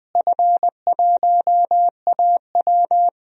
Morse code, heard as a keyed tone, about 21 wpm, 700 Hz. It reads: F1AW
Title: DXCC-100 Call Signs - 4 Characters - Repeated 2x - 20wpm